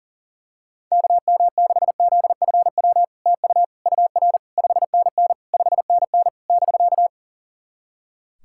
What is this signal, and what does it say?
Morse code, heard as a keyed tone, about 40 wpm, 700 Hz. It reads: KM6ZFW TU UR 5NN 5NN <BK>
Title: SOTA QSO Practice 40wpm